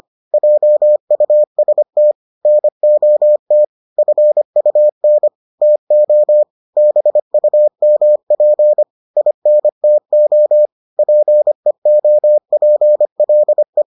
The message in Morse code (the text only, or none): JUST NOT FUN TO BUMP INTO PEOPLE